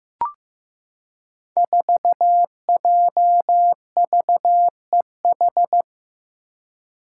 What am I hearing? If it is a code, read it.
Morse code, heard as a keyed tone, about 15 wpm, 700 Hz. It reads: 4JVEH
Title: Five Challenging Letter-Number Combinations 15wpm